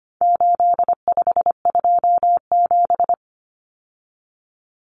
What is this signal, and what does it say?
Morse code, heard as a keyed tone, about 25 wpm, 700 Hz. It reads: 8527
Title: Common Cut Numbers - 4-Digit 25wpm